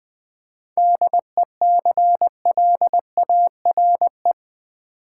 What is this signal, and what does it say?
Morse code, heard as a keyed tone, about 20 wpm, 700 Hz. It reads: DECLARE
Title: 7 Letter Words - First 3 Letters Given - 20wpm